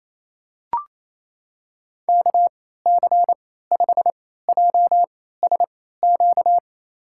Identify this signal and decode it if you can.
Morse code, heard as a keyed tone, about 28 wpm, spaced as for 16 wpm, 700 Hz. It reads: KC5JSQ
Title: USA Call Signs - 6 Characters - 3x Character Spacing - 28wpm